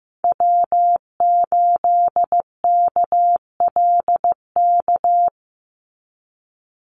Morse code, heard as a keyed tone, about 15 wpm, 700 Hz. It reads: W8KLK
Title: USA Call Signs 15wpm